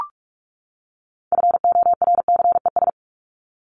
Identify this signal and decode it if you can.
Morse code, heard as a keyed tone, about 44 wpm, 700 Hz. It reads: FORCES
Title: ES:S Suffix - Top 100 Words with Base Words 45wpm